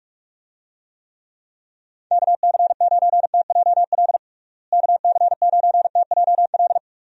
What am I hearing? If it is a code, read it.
Morse code, heard as a keyed tone, about 46 wpm, 700 Hz. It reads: KC9TJL KC9TJL
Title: USA Call Signs - 6 Characters - Repeated 3x - 45wpm